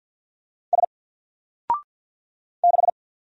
Morse code, heard as a keyed tone, about 49 wpm, 700 Hz. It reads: S 6
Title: Single Letter-Number Alt 50wpm